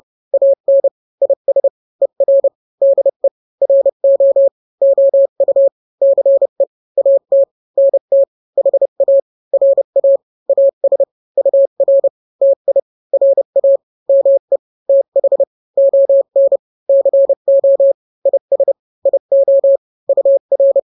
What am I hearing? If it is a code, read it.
Morse code, heard as a keyed tone, about 30 wpm, 550 Hz. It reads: AN IS ER DE RO OU CE AT NT HA RA AS UR TI RA ME TH ON CO IS IO UR
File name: n-grams / 2-grams 30 25 WPM